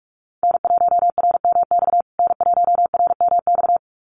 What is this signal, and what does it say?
Morse code, heard as a keyed tone, about 48 wpm, 700 Hz. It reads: N1RMX N1RMX
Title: USA Call Signs Repeated 2x 45wpm